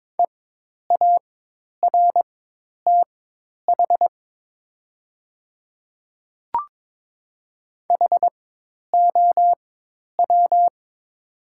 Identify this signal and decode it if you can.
Morse code, heard as a keyed tone, about 22 wpm, spaced as for 10 wpm, 690 Hz. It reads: EARTH HOW
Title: 4x Character Spacing - Top 200 Words - 22wpm